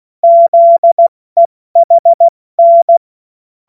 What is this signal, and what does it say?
Morse code, heard as a keyed tone, about 16 wpm, spaced as for 14 wpm, 680 Hz. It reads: ZEHN
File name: Morsekurs / Witze / Witz 05 März 2024 70 F80